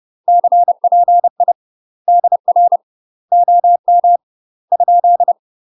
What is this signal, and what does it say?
Morse code, heard as a keyed tone, about 30 wpm, 700 Hz. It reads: CPI DR OM ?
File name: UK MASTER QSOs TRAINING / 30WPM / UK MASTER QSOs 0026